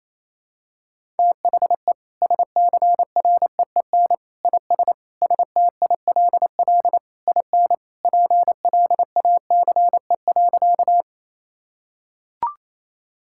Morse code, heard as a keyed tone, about 28 wpm, 700 Hz. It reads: THE SCREEN IS STILL IN PLACE.
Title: Sentences from Top 1000 Words - Review Part 6 - 28wpm